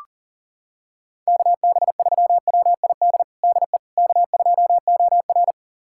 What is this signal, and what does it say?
Morse code, heard as a keyed tone, about 40 wpm, 700 Hz. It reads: KB3WID DE K2OR
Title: USA Callsign DE Callsign 40wpm